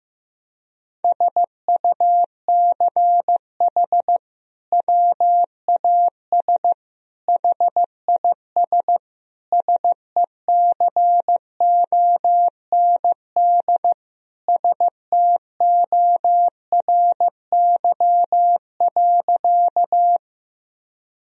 Morse code, heard as a keyed tone, about 16 wpm, 700 Hz. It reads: SUCH WAS HIS SECOND STORY.
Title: Sentences from Top 1000 Words - Review Part 1 - 15wpm